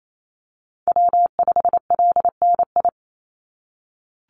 Morse code, heard as a keyed tone, about 28 wpm, 700 Hz. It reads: W5LNI
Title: Call Signs - 28wpm